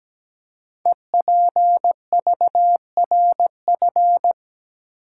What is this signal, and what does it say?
Morse code, heard as a keyed tone, about 17 wpm, 700 Hz. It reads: EPVRF
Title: Code Groups - 5 Letters - 17wpm